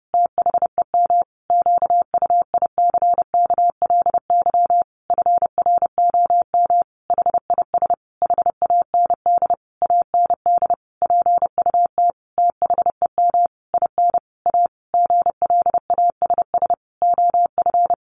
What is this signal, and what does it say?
Morse code, heard as a keyed tone, about 30 wpm, 700 Hz. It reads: THEM QUICKLY FROM HIS HAND AND PUT THEM IN A GLASS OF
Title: Sentences from Top 1300 Words 30wpm